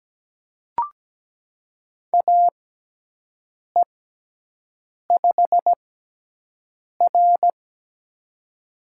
Morse code, heard as a keyed tone, about 17 wpm, spaced as for 6 wpm, 700 Hz. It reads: AE5R
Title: USA Call Signs - 6x Character Spacing - 17wpm